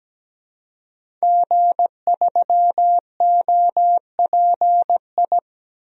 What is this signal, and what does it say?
Morse code, heard as a keyed tone, about 17 wpm, 700 Hz. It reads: G3OPI
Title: DXCC-100 Call Signs - 5 Characters - 17wpm